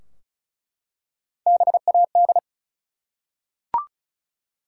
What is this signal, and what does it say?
Morse code, heard as a keyed tone, about 34 wpm, 700 Hz. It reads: BAD
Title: Three Letter Words - 35wpm